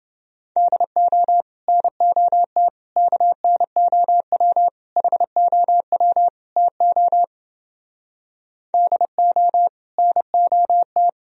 Morse code, heard as a keyed tone, about 30 wpm, 700 Hz. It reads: DO NOT KNOW HOW TO DO NOT